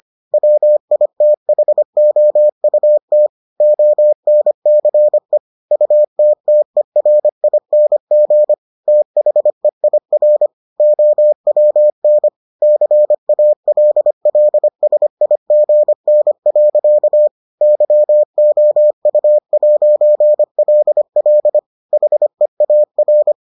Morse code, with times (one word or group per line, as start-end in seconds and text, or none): 0.3-3.3 WITHOUT
3.6-5.4 ONCE
5.7-8.5 UTTERING
8.9-10.5 THEIR
10.8-12.3 OWN
12.6-17.3 CALLSIGN.
17.6-21.6 YOU'LL
21.9-23.3 HEAR